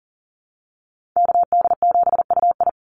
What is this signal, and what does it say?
Morse code, heard as a keyed tone, about 40 wpm, 700 Hz. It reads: KD7UI
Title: Call Signs Repeated 2x - 40wpm